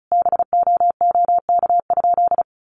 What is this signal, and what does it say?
Morse code, heard as a keyed tone, about 34 wpm, 700 Hz. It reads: BOOK?
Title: Sentences from Top 1400 Words 35wpm